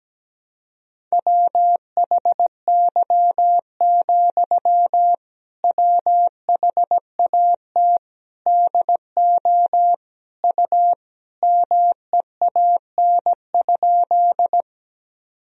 Morse code, heard as a keyed tone, about 17 wpm, 700 Hz. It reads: WHY, WHAT DO U MEAN?
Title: Sentences from Top 200 Words and Abbreviations 17wpm